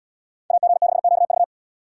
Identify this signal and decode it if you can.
Morse code, heard as a keyed tone, about 75 wpm, 700 Hz. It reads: AG6QV